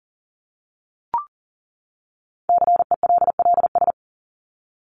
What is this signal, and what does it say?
Morse code, heard as a keyed tone, about 40 wpm, 700 Hz. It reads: CELLS
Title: ES:S Suffix - Top 100 Words with Base Words 40wpm